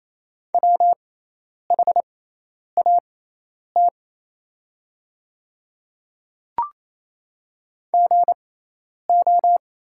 Morse code, heard as a keyed tone, about 27 wpm, spaced as for 10 wpm, 700 Hz. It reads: WHAT GO
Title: 6x Character Spacing - Top 100 Words - 28wpm